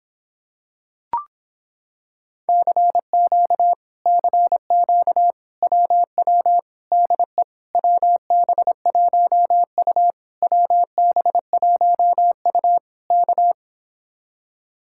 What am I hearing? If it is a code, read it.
Morse code, heard as a keyed tone, about 26 wpm, 700 Hz. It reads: CQ CQ WW DE WB1U WB1U K